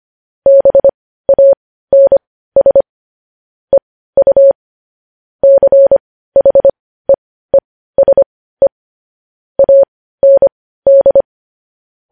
Morse code, heard as a keyed tone, about 25 wpm, spaced as for 15 wpm, 550 Hz. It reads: BANS EU CHEESE AND
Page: news headlines in morse code at 15 wpm